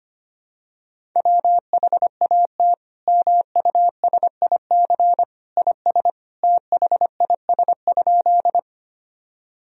Morse code, heard as a keyed tone, about 25 wpm, 700 Hz. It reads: WHAT MUSIC IS THIS?
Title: Sentences from Top 1000 Words - Review Part 1 - 25wpm